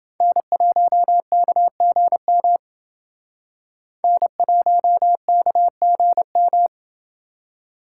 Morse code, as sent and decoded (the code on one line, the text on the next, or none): -. .---- -.- --. -- / -. .---- -.- --. --
N1KGM N1KGM